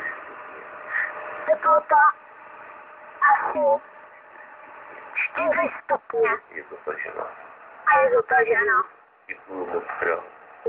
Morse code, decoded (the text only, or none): none